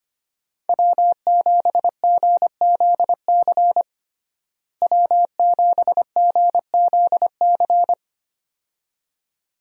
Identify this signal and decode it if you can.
Morse code, heard as a keyed tone, about 25 wpm, 700 Hz. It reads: W7GZC W7GZC
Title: USA Call Signs - 5 Characters - Repeated 2x - 25wpm